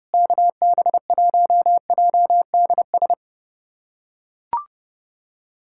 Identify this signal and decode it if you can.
Morse code, heard as a keyed tone, about 28 wpm, 700 Hz. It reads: KB1JDS
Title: USA Callsign DE Callsign 30wpm